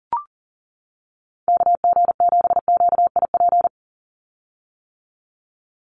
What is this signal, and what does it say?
Morse code, heard as a keyed tone, about 40 wpm, 700 Hz. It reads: KG7QIP
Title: USA Call Signs 40wpm